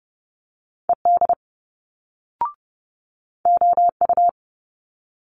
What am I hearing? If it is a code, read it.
Morse code, heard as a keyed tone, about 30 wpm, 700 Hz. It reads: ED OU
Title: Multi-letter Phonemes Quick Fire 30wpm